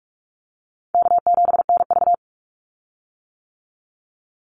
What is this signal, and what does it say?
Morse code, heard as a keyed tone, about 44 wpm, 700 Hz. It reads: K7NV
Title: Call Signs - 45wpm